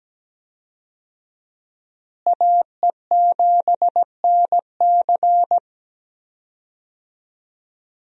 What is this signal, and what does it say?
Morse code, heard as a keyed tone, about 17 wpm, 700 Hz. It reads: AE7NC
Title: USA Call Signs - 5 Characters - Repeated 2x - 17wpm